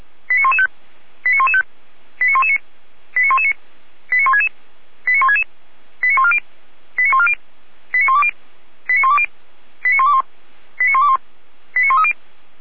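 Five-tone selective call (ZVEI), heard as quick runs of five tones, tones of about 70 ms. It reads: 89115 89115 89119 89119 89166 89166 89140 89140 89120 89120 89121 89121 89130
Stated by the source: ZVEI five-tone selective call received on an SDR.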